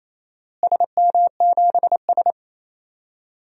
Morse code, heard as a keyed tone, about 28 wpm, 700 Hz. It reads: SM7S
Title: DXCC-100 Call Signs - 4 Characters - Repeated 2x - 28wpm